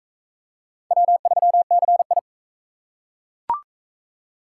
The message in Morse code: W3CI

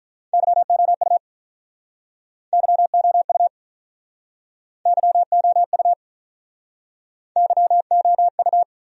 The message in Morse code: YOU YOU YOU YOU

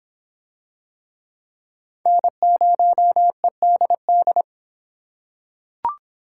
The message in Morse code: N0EDD

